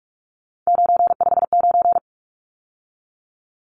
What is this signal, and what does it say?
Morse code, heard as a keyed tone, about 44 wpm, 700 Hz. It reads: 959
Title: Numbers - 3-Digit 45wpm